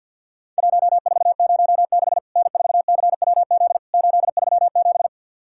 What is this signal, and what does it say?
Morse code, heard as a keyed tone, about 50 wpm, 700 Hz. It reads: 1406 N4CWZ 837
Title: Worked All Europe DX Contest - QTC Exchange - 50wpm